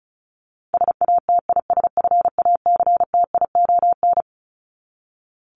Morse code, heard as a keyed tone, about 34 wpm, 700 Hz. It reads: SATISFACTION
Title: Top 1751-2000 Words 35wpm